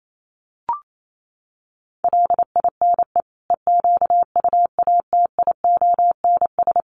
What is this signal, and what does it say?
Morse code, heard as a keyed tone, about 28 wpm, 700 Hz. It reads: LINE EQUATIONS
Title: Sets of 5 Words 28wpm